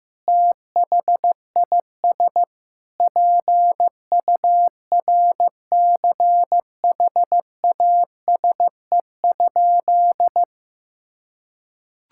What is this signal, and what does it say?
Morse code, heard as a keyed tone, about 15 wpm, 700 Hz. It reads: THIS PURCHASE?
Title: Sentences from Top 1000 Words - Review Part 5 - 15wpm